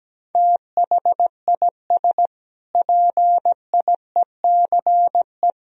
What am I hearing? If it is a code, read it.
Morse code, heard as a keyed tone, about 17 wpm, 700 Hz. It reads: THIS PIECE
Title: Sentences from Top 1000 Words - Review Part 2 - 17wpm